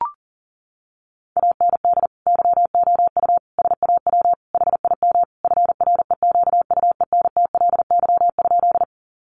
Morse code, heard as a keyed tone, about 40 wpm, 700 Hz. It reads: AND YOU SAW HIM FREQUENTLY?
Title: Sentences from Top 1300 Words 40wpm